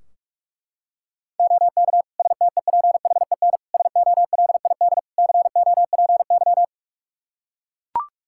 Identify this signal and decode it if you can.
Morse code, heard as a keyed tone, about 42 wpm, 700 Hz. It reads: OK STEPHEN SOLID COPY
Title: K1USN SST QSO Practice 45wpm